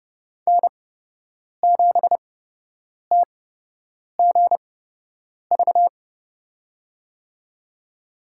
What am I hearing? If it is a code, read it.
Morse code, heard as a keyed tone, about 30 wpm, spaced as for 8 wpm, 700 Hz. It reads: N7TGV
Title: USA Call Signs - 5 Characters - 8x Character Spacing - 30wpm